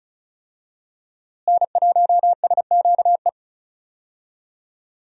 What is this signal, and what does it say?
Morse code, heard as a keyed tone, about 36 wpm, 700 Hz. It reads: N1SQE
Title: USA Call Signs - 5 Characters - Repeated 2x - 35wpm